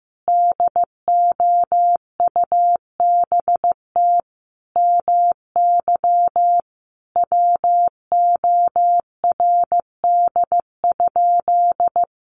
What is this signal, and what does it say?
Morse code, heard as a keyed tone, about 15 wpm, 700 Hz. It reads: DOUBT MY WORD?
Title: Sentences from Top 1500 Words 15wpm